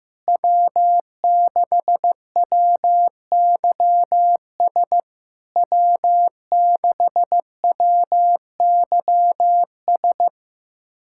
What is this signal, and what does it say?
Morse code, heard as a keyed tone, about 15 wpm, 700 Hz. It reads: W6WYS W6WYS
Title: SOTA QSO Practice 15wpm